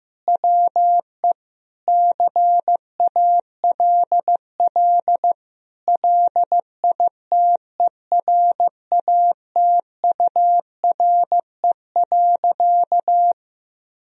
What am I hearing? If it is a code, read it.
Morse code, heard as a keyed tone, about 15 wpm, 700 Hz. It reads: WE CALL LITERATURE.
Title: Sentences from Top 1000 Words - Review Part 3 - 15wpm